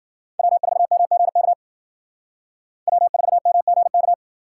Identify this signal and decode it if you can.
Morse code, heard as a keyed tone, about 55 wpm, 700 Hz. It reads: W4MGK W4MGK